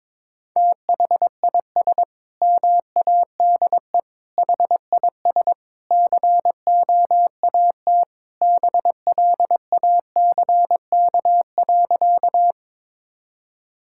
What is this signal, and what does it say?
Morse code, heard as a keyed tone, about 22 wpm, 700 Hz. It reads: THIS MADE HIS COAT BLACK.
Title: Sentences from Top 1600 Words 22wpm